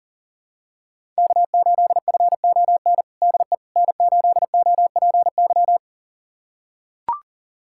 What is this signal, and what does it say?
Morse code, heard as a keyed tone, about 40 wpm, 700 Hz. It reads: K8FON DE N8OPY
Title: USA Callsign DE Callsign - 5 Characters - 40wpm